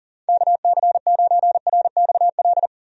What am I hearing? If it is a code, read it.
Morse code, heard as a keyed tone, about 40 wpm, 700 Hz. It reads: KC9RXL
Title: USA Call Signs - 6 Characters - 40wpm